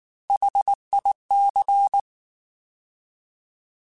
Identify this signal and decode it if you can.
Morse code, heard as a keyed tone, about 19 wpm, 800 Hz. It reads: HIC